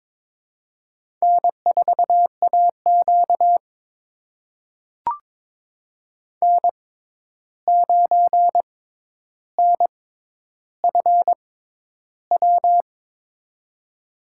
Morse code, heard as a keyed tone, about 21 wpm, spaced as for 10 wpm, 700 Hz. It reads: N4AQ N9NFW
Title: USA Call Signs - 6x Character Spacing - 22wpm